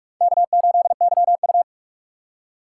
Morse code, heard as a keyed tone, about 41 wpm, 700 Hz. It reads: K8YU
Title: USA Call Signs - 4 Characters - 4x Character Spacing - 45wpm